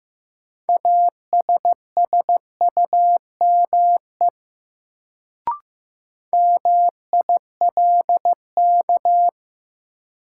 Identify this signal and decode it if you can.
Morse code, heard as a keyed tone, about 15 wpm, 700 Hz. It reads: ASSUME MILK
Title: Sentences from Top 1500 Words - New Word Drill Alt - 15wpm